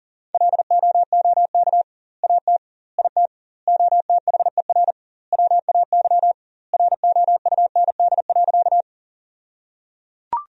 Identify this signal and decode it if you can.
Morse code, heard as a keyed tone, about 37 wpm, 700 Hz. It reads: LOOK AT IT OTHER WAY ROUND.
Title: 2x Word-Spacing - Sentences from Top 200 Words - 40wpm